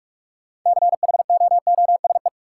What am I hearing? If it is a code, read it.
Morse code, heard as a keyed tone, about 46 wpm, 700 Hz. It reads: CHOOSE
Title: Sentences from Top 700 Words - New Word Drill - 45wpm